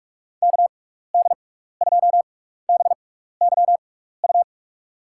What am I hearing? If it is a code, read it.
Morse code, heard as a keyed tone, about 43 wpm, spaced as for 15 wpm, 700 Hz. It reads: KD2BYU
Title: USA Call Signs - 6 Characters - 6x Character Spacing - 45wpm